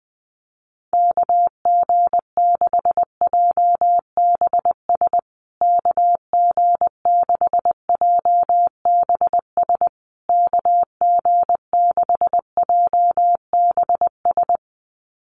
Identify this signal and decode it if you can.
Morse code, heard as a keyed tone, about 20 wpm, 700 Hz. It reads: KG6JBS KG6JBS KG6JBS
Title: USA Call Signs Repeated 3x 20wpm